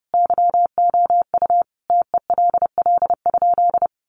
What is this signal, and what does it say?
Morse code, heard as a keyed tone, about 30 wpm, 700 Hz. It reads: YOU TELL?
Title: Sentences from Top 300 Words 30wpm